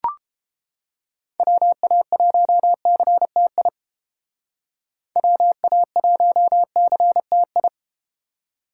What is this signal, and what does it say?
Morse code, heard as a keyed tone, about 30 wpm, 700 Hz. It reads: WA1CTI WA1CTI